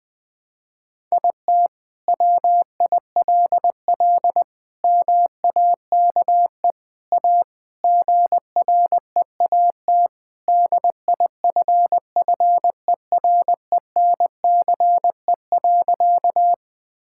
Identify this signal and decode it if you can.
Morse code, heard as a keyed tone, about 20 wpm, 700 Hz. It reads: IT WILL MAKE A GREAT DIFFERENCE.
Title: Sentences from Top 1000 Words - Review Part 5 - 20wpm